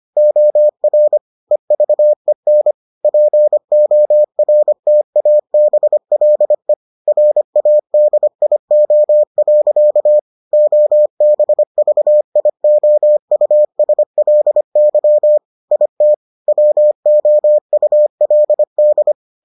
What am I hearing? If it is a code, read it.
Morse code, heard as a keyed tone, about 25 wpm, 600 Hz. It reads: OR EVEN PORTABLE RADIO. OBVIOUSLY IT WOULD